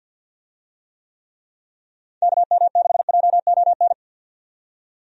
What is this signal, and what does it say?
Morse code, heard as a keyed tone, about 45 wpm, 700 Hz. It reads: KM6JON
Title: USA Call Signs - 6 Characters - 50wpm